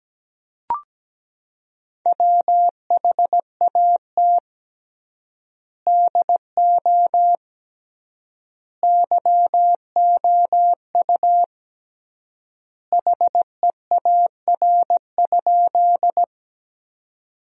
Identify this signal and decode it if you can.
Morse code, heard as a keyed tone, about 17 wpm, 700 Hz. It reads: WHAT DO YOU HEAR?
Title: 3x Word-Spacing - Sentences from Top 300 Words - 17wpm